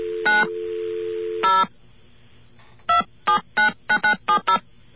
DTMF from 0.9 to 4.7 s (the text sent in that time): 0 2799970